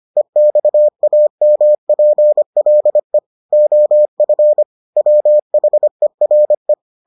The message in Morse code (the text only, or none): EXAMPLE OF WHERE